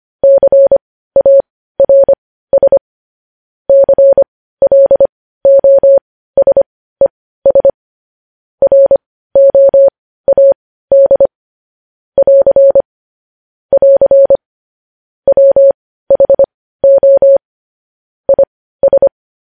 Morse code, heard as a keyed tone, about 25 wpm, spaced as for 15 wpm, 550 Hz. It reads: CARS CLOSES ROAD <AR> <AR> WHO IS
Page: news-headlines-in-morse-code-at-15-wpm